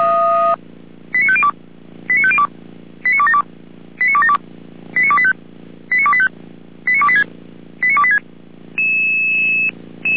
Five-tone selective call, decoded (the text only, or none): 89502 89502 89272 89272 89285 89285 89286 89286